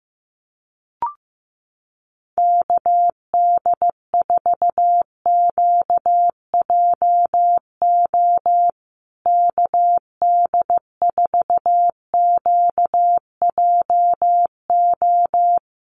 Morse code, heard as a keyed tone, about 15 wpm, 700 Hz. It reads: KD4QJO KD4QJO
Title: USA Call Signs Repeated 2x 15wpm